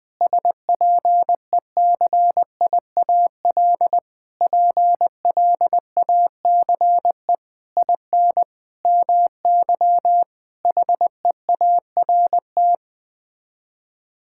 Morse code, heard as a keyed tone, about 20 wpm, 700 Hz. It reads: SPECIAL PLACE IN MY HEART
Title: Sets of 9 Words - 20wpm